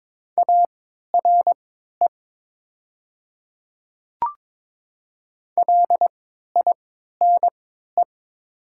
Morse code, heard as a keyed tone, about 21 wpm, spaced as for 12 wpm, 700 Hz. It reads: ARE LINE
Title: 3x Character Spacing - Top 200 Words - 22wpm